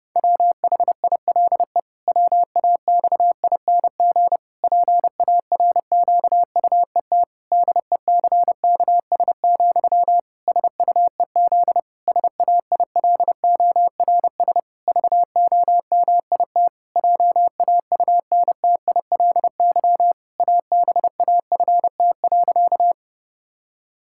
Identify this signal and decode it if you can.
Morse code, heard as a keyed tone, about 30 wpm, 700 Hz. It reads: WHILE WAXING PARQUET DECKS, SUEZ SAILORS VOMIT JAUNTILY ABAFT.
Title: Pangram Challenge v2 - 30wpm